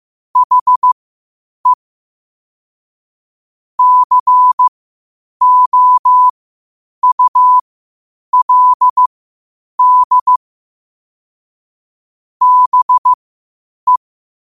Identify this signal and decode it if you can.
Morse code, heard as a keyed tone, about 15 wpm, spaced as for 8 wpm, 1000 Hz. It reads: HE COULD BE